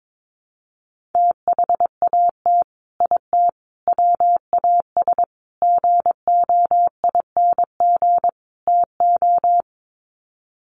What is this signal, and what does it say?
Morse code, heard as a keyed tone, about 22 wpm, 700 Hz. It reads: THAT IT WAS GOING TO
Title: Sets of 5 Words Encore2 22wpm